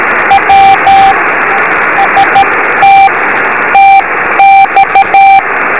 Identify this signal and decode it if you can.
Morse code, heard as a keyed tone, about 13 wpm, spaced as for 11 wpm, 780 Hz, with radio noise. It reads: W ST TX
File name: beacons / w5rp-b